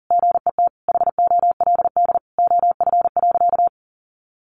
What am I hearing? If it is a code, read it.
Morse code, heard as a keyed tone, about 40 wpm, 700 Hz. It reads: GET HOLD OF.
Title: A Sound Of Thunder 40wpm